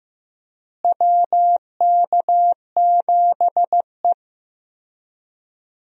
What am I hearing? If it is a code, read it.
Morse code, heard as a keyed tone, about 15 wpm, 700 Hz. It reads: WK7E
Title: USA Call Signs - 4 Characters - Repeated 3x - 15wpm